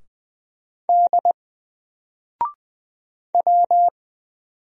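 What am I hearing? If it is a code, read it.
Morse code, heard as a keyed tone, about 20 wpm, 700 Hz. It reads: D W
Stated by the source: Single letter and number Morse drill at 20 wpm, from N4LCD.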